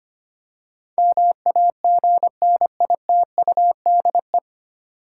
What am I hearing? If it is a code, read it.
Morse code, heard as a keyed tone, about 25 wpm, 700 Hz. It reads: MAGNITUDE
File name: practice / 25wpm / 9 Letter Words - First 3 Letters Given - 25wpm